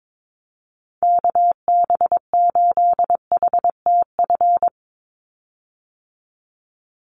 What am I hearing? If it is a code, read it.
Morse code, heard as a keyed tone, about 22 wpm, 700 Hz. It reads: KB8HTF
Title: USA Call Signs 22wpm